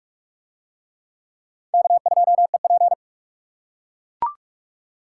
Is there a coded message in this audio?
Yes: Morse code from 1.7 to 2.9 s, heard as a keyed tone, about 43 wpm, 700 Hz. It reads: K2EP